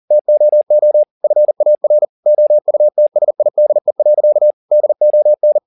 Morse code, heard as a keyed tone, about 40 wpm, 600 Hz. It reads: TOO FAR OUTSIDE. DON
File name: the old man and the sea 2 40wpm